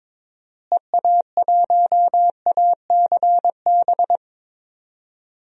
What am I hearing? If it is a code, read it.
Morse code, heard as a keyed tone, about 21 wpm, 700 Hz. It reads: EA1ACB
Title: DXCC-100 Call Signs - 6 Characters - Repeated 3x - 22wpm